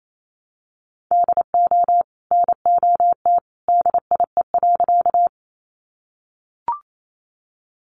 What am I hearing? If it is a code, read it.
Morse code, heard as a keyed tone, about 28 wpm, 700 Hz. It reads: DO NOT DIE.
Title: Sentences from Top 1300 Words 28wpm